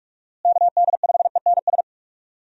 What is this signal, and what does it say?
Morse code, heard as a keyed tone, about 46 wpm, 700 Hz. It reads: KD5ENS
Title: USA Call Signs - 6 Characters - Repeated 2x - 45wpm